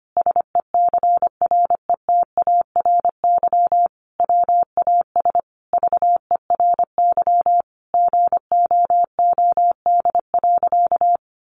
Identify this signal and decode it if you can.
Morse code, heard as a keyed tone, about 25 wpm, 700 Hz. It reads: SECRETARY WAS VERY GOOD.
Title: Sentences from Top 1500 Words 25wpm